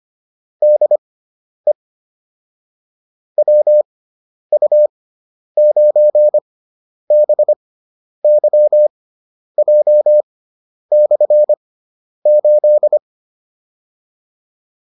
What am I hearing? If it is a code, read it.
Morse code, heard as a keyed tone, about 25 wpm, spaced as for 10 wpm, 600 Hz. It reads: DE WU9BYJ/8
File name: QSO306 10